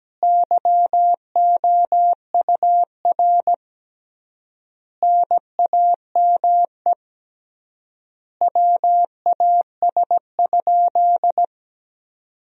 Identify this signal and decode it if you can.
Morse code, heard as a keyed tone, about 17 wpm, 700 Hz. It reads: YOUR NAME WAS?
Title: 3x Word-Spacing - Sentences from Top 200 Words - 17wpm